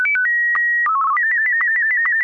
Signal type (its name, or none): SSTV (slow-scan television)